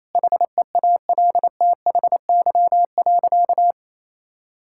Morse code, heard as a keyed tone, about 28 wpm, 700 Hz. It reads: HEALTHY.
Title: Sentences from Top 1000 Words - Review Part 5 - 28wpm